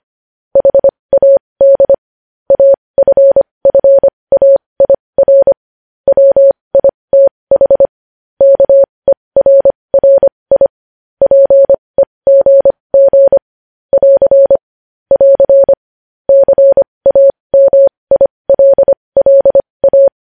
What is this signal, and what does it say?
Morse code, heard as a keyed tone, about 25 wpm, spaced as for 20 wpm, 550 Hz. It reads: HAD AFFAIR WITH KERRI PEGG <AR> <AR> CAMILLA
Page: news headlines in morse code at 20 wpm